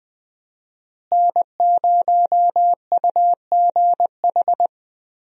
Morse code, heard as a keyed tone, about 21 wpm, 700 Hz. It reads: N0UGH
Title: USA Call Signs - 5 Characters - Repeated 3x - 20wpm